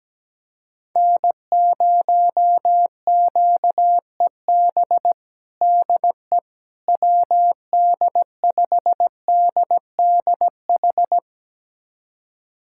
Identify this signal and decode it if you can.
Morse code, heard as a keyed tone, about 17 wpm, 700 Hz. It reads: N0QEB DE WD5DDH
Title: USA Callsign DE Callsign 17wpm